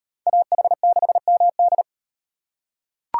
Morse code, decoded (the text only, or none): AH6MD